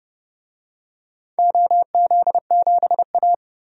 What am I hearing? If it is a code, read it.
Morse code, heard as a keyed tone, about 30 wpm, 700 Hz. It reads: OZ7A
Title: DXCC-100 Call Signs - 4 Characters - 30wpm